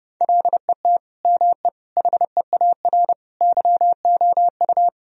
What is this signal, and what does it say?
Morse code, heard as a keyed tone, about 30 wpm, 700 Hz. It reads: LET ME HEAR YOU
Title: Sentences from Top 1000 Words - Review Part 2 - 30wpm